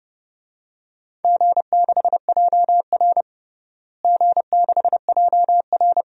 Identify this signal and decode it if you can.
Morse code, heard as a keyed tone, about 30 wpm, 700 Hz. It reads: G6JR G6JR
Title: DXCC-100 Call Signs - 4 Characters - Repeated 3x - 30wpm